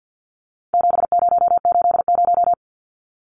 Morse code, heard as a keyed tone, about 51 wpm, 700 Hz. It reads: 7080
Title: Numbers - 4-Digit 50wpm